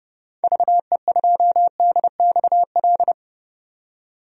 Morse code, heard as a keyed tone, about 30 wpm, 700 Hz. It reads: VE2DXL
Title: DXCC-100 Call Signs - 6 Characters - 30wpm